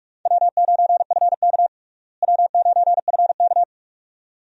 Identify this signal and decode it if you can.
Morse code, heard as a keyed tone, about 41 wpm, 700 Hz. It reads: W9FK W9FK
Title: USA Call Signs - 4 Characters - Repeated 2x - 45wpm